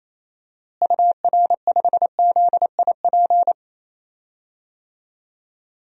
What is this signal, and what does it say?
Morse code, heard as a keyed tone, about 28 wpm, 700 Hz. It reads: UR5ZIP
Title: DXCC-100 Call Signs - 6 Characters - Repeated 3x - 28wpm